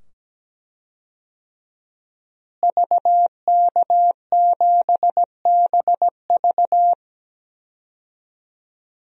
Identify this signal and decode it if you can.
Morse code, heard as a keyed tone, about 17 wpm, 700 Hz. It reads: VK7BV
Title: DXCC-100 Call Signs - 5 Characters - Repeated 2x - 17wpm